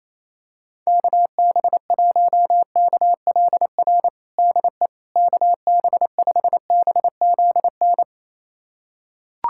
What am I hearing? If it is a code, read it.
Morse code, heard as a keyed tone, about 28 wpm, 700 Hz. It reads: KB1KLR DE KB5BZN
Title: USA Callsign DE Callsign - 6 Characters - 28wpm